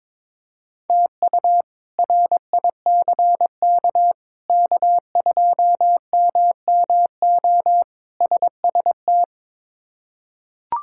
Morse code, heard as a keyed tone, about 22 wpm, 700 Hz. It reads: TU RICK K2MMO SST
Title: K1USN SST QSO Practice - 22wpm